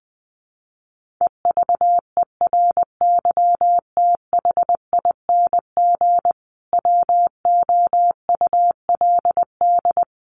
Morse code, heard as a keyed tone, about 20 wpm, 700 Hz. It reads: EVERYTHING WOULD
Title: Sentences from Top 1400 Words 20wpm